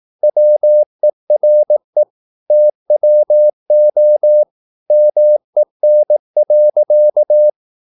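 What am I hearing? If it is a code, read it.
Morse code, heard as a keyed tone, about 18 wpm, 600 Hz. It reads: WERE TWO MEN.
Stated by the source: Morse practice at 18 wpm, English text, from CW med Gnister.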